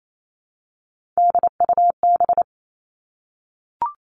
Morse code, heard as a keyed tone, about 28 wpm, 700 Hz. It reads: DUB
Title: Three Challenging Letter-Number Combinations 28wpm